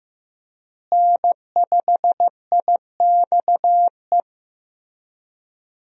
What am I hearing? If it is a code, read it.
Morse code, heard as a keyed tone, about 15 wpm, 700 Hz. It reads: N5IXE
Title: USA Call Signs - 5 Characters - Repeated 3x - 15wpm